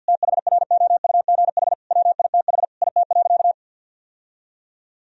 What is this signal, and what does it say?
Morse code, heard as a keyed tone, about 50 wpm, 700 Hz. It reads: THROUGH WITH IT.
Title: Sentences from Top 1000 Words - Review Part 3 - 50wpm